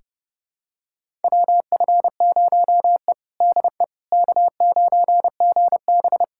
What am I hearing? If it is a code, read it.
Morse code, heard as a keyed tone, about 32 wpm, 700 Hz. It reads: WF0E DE K9GB
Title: USA Callsign DE Callsign - 4 Characters - 30wpm